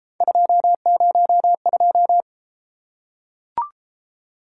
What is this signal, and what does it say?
Morse code, heard as a keyed tone, about 28 wpm, 700 Hz. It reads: 202